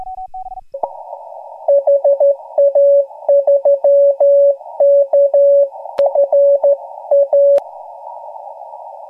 Morse code, heard as a keyed tone, about 13 wpm, 560 Hz. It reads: HA3KFA
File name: AUDIO2018 / HG1G CW